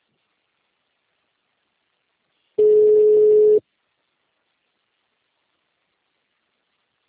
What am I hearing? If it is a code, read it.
Ringback tone (Europe, 425 Hz).
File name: out- -2015- - - .1161